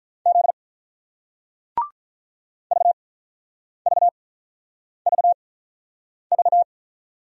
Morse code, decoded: Z VVVV

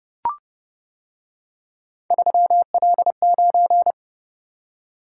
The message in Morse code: 3L9

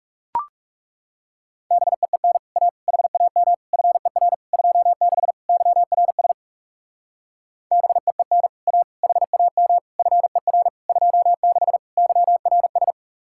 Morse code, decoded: BEEN A HAM FER 26 YRS BEEN A HAM FER 26 YRS

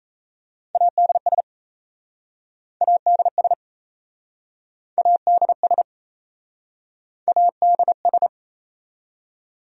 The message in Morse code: ADS ADS ADS ADS

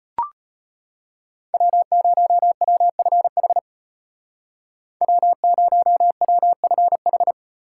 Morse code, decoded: W0WFH W0WFH

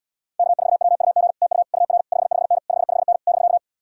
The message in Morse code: K4WUG TU UR 55N 55N <BK>

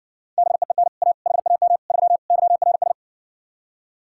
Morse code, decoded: BEEN A HAM 3 YRS

